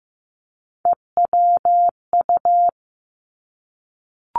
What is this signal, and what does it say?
Morse code, heard as a keyed tone, about 15 wpm, 700 Hz. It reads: EWU
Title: Three Challenging Letter-Number Combinations 15wpm